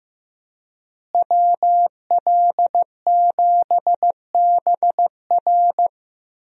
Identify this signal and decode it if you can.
Morse code, heard as a keyed tone, about 15 wpm, 700 Hz. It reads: WL7BR
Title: USA Call Signs - 5 Characters - Repeated 3x - 15wpm